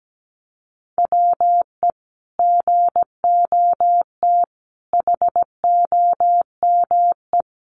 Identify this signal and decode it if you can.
Morse code, heard as a keyed tone, about 17 wpm, 700 Hz. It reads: WE GOT HOME
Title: Sentences from Top 900 Words 17wpm